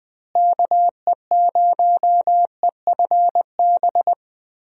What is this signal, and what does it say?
Morse code, heard as a keyed tone, about 21 wpm, 700 Hz. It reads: KE0EFB
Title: USA Call Signs - 6 Characters - Repeated 3x - 20wpm